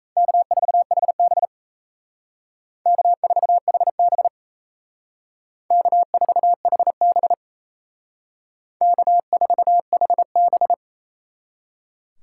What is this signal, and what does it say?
Morse code, heard as a keyed tone, about 37 wpm, 700 Hz. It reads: K4HB K4HB K4HB K4HB